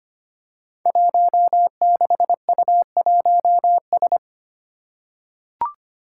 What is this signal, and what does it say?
Morse code, heard as a keyed tone, about 25 wpm, 700 Hz. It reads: 16U1S
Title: Five Challenging Letter-Number Combinations 25wpm